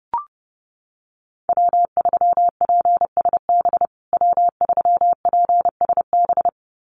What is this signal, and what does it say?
Morse code, heard as a keyed tone, about 30 wpm, 700 Hz. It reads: W3PSB W3PSB
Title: USA Call Signs Repeated 2x 30wpm